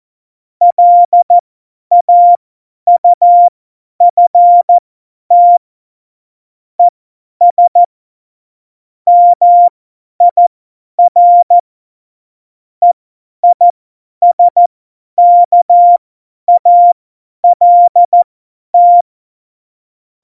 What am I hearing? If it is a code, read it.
Morse code, heard as a keyed tone, about 14 wpm, spaced as for 10 wpm, 700 Hz. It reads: LAUFT ES MIR EISKALT
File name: Morsekurs / Witze / Witz 05. September 17, Tempo 50